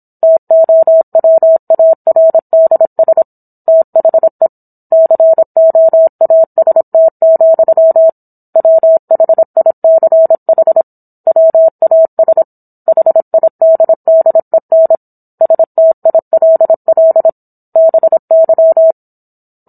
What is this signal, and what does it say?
Morse code, heard as a keyed tone, about 26 wpm, 650 Hz. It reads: TOWARDS THE COAST, WHICH WAS HIDDEN STILL BY